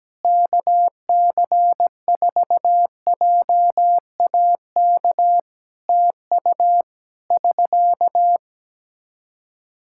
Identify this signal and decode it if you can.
Morse code, heard as a keyed tone, about 17 wpm, 700 Hz. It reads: KC4JAK TU <SK>
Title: Practice QSO - Exchange 4 - Ending - 17wpm